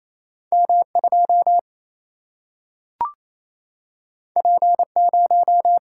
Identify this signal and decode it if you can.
Morse code, heard as a keyed tone, about 28 wpm, 700 Hz. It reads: M2 P0
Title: Single-Letter Single-Number 28wpm